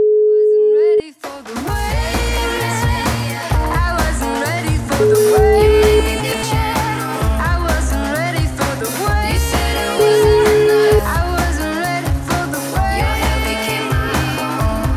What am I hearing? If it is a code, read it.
Ringback tone (Europe, 425 Hz), with noise.